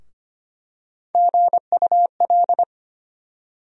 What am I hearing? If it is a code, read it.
Morse code, heard as a keyed tone, about 25 wpm, 700 Hz. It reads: GUL
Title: USA Call Sign Postfixes - 3 Letters 25wpm